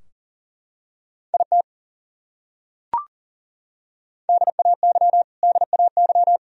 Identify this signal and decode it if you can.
Morse code, heard as a keyed tone, about 40 wpm, 700 Hz. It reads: IT DAY DAY
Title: Top 100 Words Repeated 40wpm